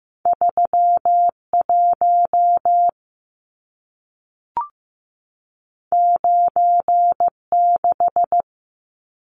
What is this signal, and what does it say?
Morse code, heard as a keyed tone, about 15 wpm, 700 Hz. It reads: 31 96
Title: Numbers - 2-Digit 15wpm